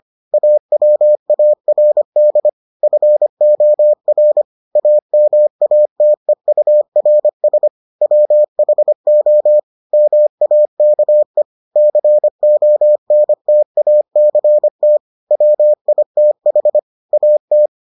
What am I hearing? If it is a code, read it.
Morse code, heard as a keyed tone, about 25 wpm, 600 Hz. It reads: AWARD FOR AMATEURS WHO MAKE CONTACT WITH AT